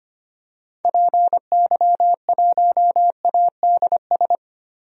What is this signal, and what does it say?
Morse code, heard as a keyed tone, about 25 wpm, 700 Hz. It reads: PY1ADS
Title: DXCC-100 Call Signs - 6 Characters - 25wpm